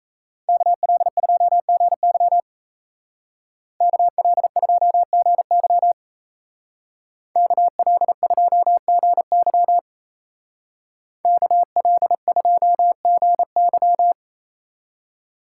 Morse code, heard as a keyed tone, about 34 wpm, 700 Hz. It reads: KL2GY KL2GY KL2GY KL2GY
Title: Speed Racing - USA 2x2 Call Signs 28wpm